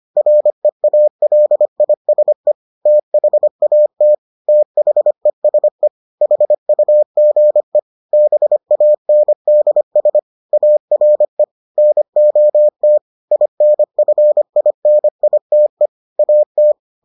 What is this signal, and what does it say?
Morse code, heard as a keyed tone, about 25 wpm, 600 Hz. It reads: REALISE THAT THESE HUGE BANDS ARE NOT INFINITE AT